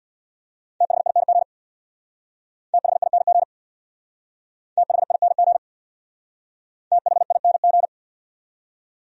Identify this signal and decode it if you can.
Morse code, heard as a keyed tone, about 60 wpm, 700 Hz. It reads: THING THING THING THING
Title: Speed-Racing Top 200 Words 50wpm